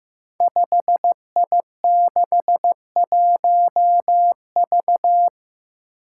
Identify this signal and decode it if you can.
Morse code, heard as a keyed tone, about 15 wpm, 700 Hz. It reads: 5I61V
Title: Five Challenging Letter-Number Combinations 15wpm